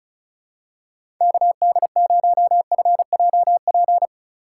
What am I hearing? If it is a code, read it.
Morse code, heard as a keyed tone, about 36 wpm, 700 Hz. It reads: KD0FJP
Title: USA Call Signs - 6 Characters - 2x Character Spacing - 35wpm